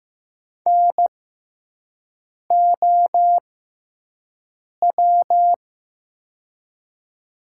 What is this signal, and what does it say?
Morse code, heard as a keyed tone, about 15 wpm, spaced as for 5 wpm, 700 Hz. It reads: NOW
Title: 6x Character Spacing - Top 100 Words - 15wpm